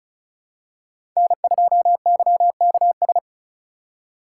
Morse code, heard as a keyed tone, about 36 wpm, 700 Hz. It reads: N2YKS
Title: USA Call Signs - 5 Characters - 3x Character Spacing - 35wpm